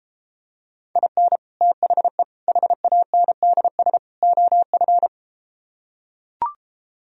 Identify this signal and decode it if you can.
Morse code, heard as a keyed tone, about 33 wpm, 700 Hz. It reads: IN THE HANDS OF